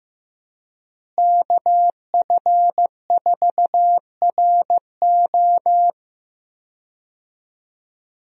Morse code, heard as a keyed tone, about 16 wpm, 700 Hz. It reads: KF4RO
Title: USA Call Signs - 5 Characters - Repeated 2x - 15wpm